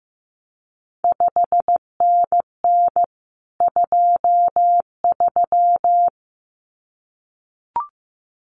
Morse code, heard as a keyed tone, about 15 wpm, 700 Hz. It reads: 5NN 23
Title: CQ WW DX Contest Exchange Encore - 15wpm